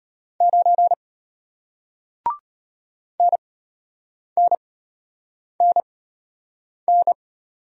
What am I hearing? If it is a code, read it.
Morse code, heard as a keyed tone, about 33 wpm, spaced as for 8 wpm, 700 Hz. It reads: 9 NNNN